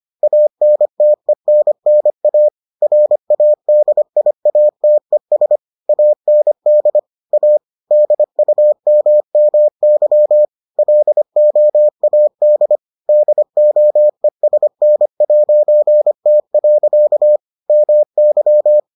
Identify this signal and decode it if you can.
Morse code, heard as a keyed tone, about 25 wpm, 600 Hz. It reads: ANTENNA RADIATES AND A DUMMY LOAD DOESN'T. MY